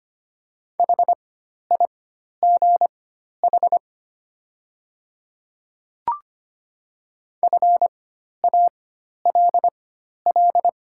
Morse code, heard as a keyed tone, about 25 wpm, spaced as for 12 wpm, 690 Hz. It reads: HIGH FALL
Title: 4x Character Spacing - Top 500 Words - 25wpm